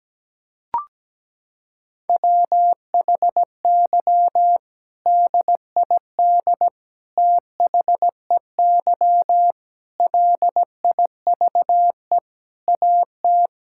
Morse code, heard as a keyed tone, about 17 wpm, 700 Hz. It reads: WHY DID THEY LIVE AT
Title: Sentences from Top 1000 Words - Review Part 6 - 17wpm